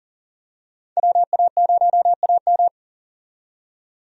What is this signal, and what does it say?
Morse code, heard as a keyed tone, about 40 wpm, 700 Hz. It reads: WA0AM
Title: USA Call Signs - 5 Characters - 40wpm